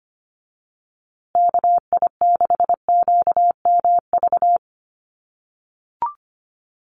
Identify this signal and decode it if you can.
Morse code, heard as a keyed tone, about 25 wpm, 700 Hz. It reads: KI6QMV
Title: USA Call Signs Repeated 3x 25wpm